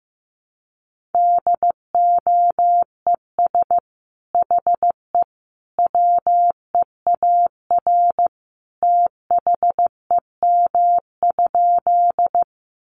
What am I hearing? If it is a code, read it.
Morse code, heard as a keyed tone, about 15 wpm, 700 Hz. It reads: DOES HE WEAR THEM?
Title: Sentences from Top 1500 Words 15wpm